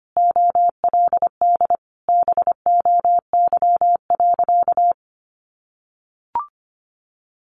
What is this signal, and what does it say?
Morse code, heard as a keyed tone, about 25 wpm, 700 Hz. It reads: OLD BOY.
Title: Sentences from Top 1500 Words 25wpm